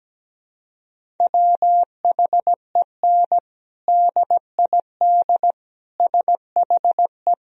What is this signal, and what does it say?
Morse code, heard as a keyed tone, about 17 wpm, 700 Hz. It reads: WHEN DID SHE
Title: Sentences from Top 1000 Words - Review Part 1 - 17wpm